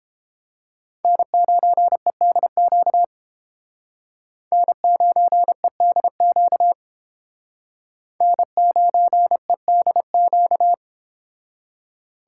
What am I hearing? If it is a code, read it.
Morse code, heard as a keyed tone, about 29 wpm, 700 Hz. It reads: N9EDQ N9EDQ N9EDQ